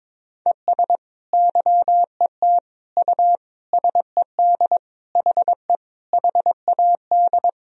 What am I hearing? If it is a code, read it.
Morse code, heard as a keyed tone, about 22 wpm, 700 Hz. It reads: ES YET U SED HE HAD
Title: Sentences from Top 500 Words and Abbreviations 22wpm